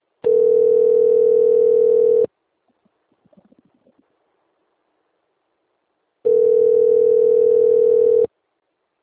Ringback tone (North America), 440 + 480 Hz.
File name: rg-1600-unknown- - - .4918